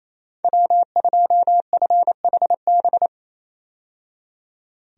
Morse code, heard as a keyed tone, about 28 wpm, 700 Hz. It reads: W2FHB
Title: USA Call Signs - 5 Characters - 28wpm